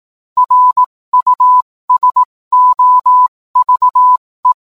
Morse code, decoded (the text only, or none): RUSOVE